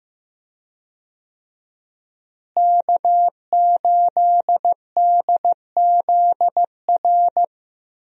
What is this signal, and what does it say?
Morse code, heard as a keyed tone, about 15 wpm, 700 Hz. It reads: K8DZR
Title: USA Call Signs - 5 Characters - Repeated 2x - 15wpm